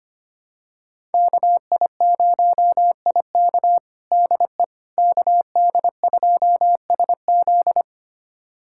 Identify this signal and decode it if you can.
Morse code, heard as a keyed tone, about 25 wpm, 700 Hz. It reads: KI0IK DE KD2SZ
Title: USA Callsign DE Callsign - 5 Characters - 25wpm